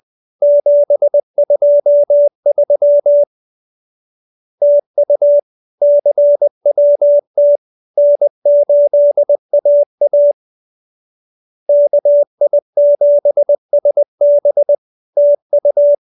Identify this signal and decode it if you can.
Morse code, heard as a keyed tone, about 20 wpm, 580 Hz. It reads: 723 TU CWT N8AA KI7SB TU
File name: CWT-201-20